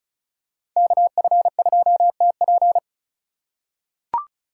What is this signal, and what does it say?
Morse code, heard as a keyed tone, about 36 wpm, 700 Hz. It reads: KF2TP
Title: K1USN SST QSO Practice - 35wpm